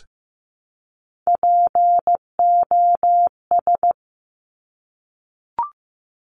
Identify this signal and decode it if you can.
Morse code, heard as a keyed tone, about 15 wpm, 700 Hz. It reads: POS
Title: Top Three Letter Combinations in Words 15wpm